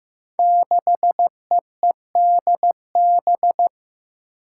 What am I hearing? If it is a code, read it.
Morse code, heard as a keyed tone, about 16 wpm, 700 Hz. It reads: 6EEDB
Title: Five Challenging Letter-Number Combinations 15wpm